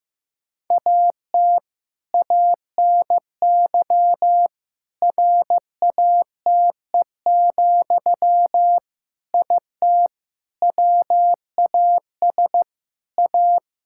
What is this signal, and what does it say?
Morse code, heard as a keyed tone, about 15 wpm, 700 Hz. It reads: AT ANY RATE, IT WAS A
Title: Sentences from Top 1600 Words 15wpm